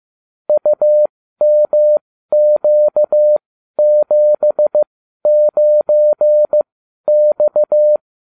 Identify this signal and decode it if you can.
Morse code, heard as a keyed tone, about 15 wpm, spaced as for 12 wpm, 600 Hz. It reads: UMQ79X